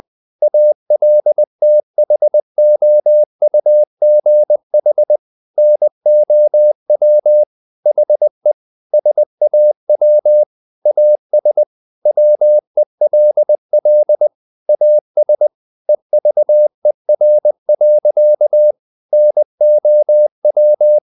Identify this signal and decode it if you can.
Morse code, heard as a keyed tone, about 20 wpm, 600 Hz. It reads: ALTHOUGH NOW HE SAW AS WELL AS EVER. NOW
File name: the old man and the sea part 2 20wpm